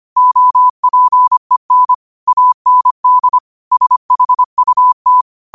Morse code, heard as a keyed tone, about 25 wpm, 1000 Hz. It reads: OPEN AND SHUT
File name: qotdfiles / 25wpm